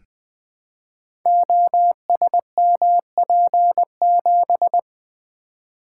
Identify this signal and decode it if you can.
Morse code, heard as a keyed tone, about 21 wpm, 700 Hz. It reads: OSMP7
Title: Code Groups - 5 Letter:Numbers - 20wpm